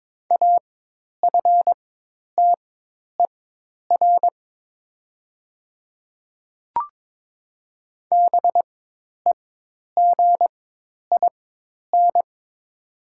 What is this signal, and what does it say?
Morse code, heard as a keyed tone, about 22 wpm, spaced as for 10 wpm, 700 Hz. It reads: AFTER BEGIN
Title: 4x Character Spacing - Top 500 Words - 22wpm